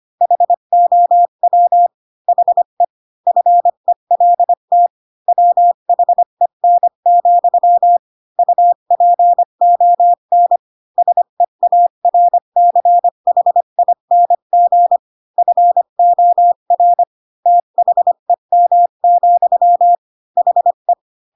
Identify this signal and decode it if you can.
Morse code, heard as a keyed tone, about 25 wpm, 700 Hz. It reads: HOW HE FELT WHEN, UPON SEARCHING FOR THEM, HE
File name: Pinocchio / Pinocchio ch3 20 25 ik1hge